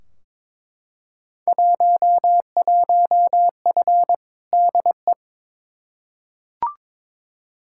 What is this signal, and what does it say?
Morse code, heard as a keyed tone, about 22 wpm, 700 Hz. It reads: 11F DE
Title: Field Day Exchanges Encore - 22wpm